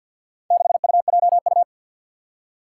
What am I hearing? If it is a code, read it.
Morse code, heard as a keyed tone, about 50 wpm, 700 Hz. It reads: 6UJU
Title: Four Challenging Letter-Number Combinations 50wpm